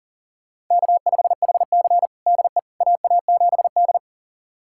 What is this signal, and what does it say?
Morse code, heard as a keyed tone, about 40 wpm, 700 Hz. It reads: K5HC DE AA7D
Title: USA Callsign DE Callsign - 4 Characters - 40wpm